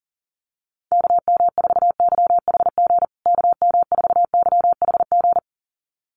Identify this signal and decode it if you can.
Morse code, heard as a keyed tone, about 40 wpm, 700 Hz. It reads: KM4YHG KM4YHG
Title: USA Call Signs Repeated 2x 40wpm